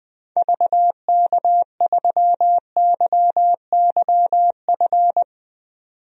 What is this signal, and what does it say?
Morse code, heard as a keyed tone, about 20 wpm, 700 Hz. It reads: VK3YYF
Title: DXCC-100 Call Signs - 6 Characters - 20wpm